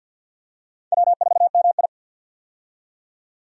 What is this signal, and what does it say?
Morse code, heard as a keyed tone, about 52 wpm, 700 Hz. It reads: W4MI
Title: USA Call Signs - 4 Characters - 50wpm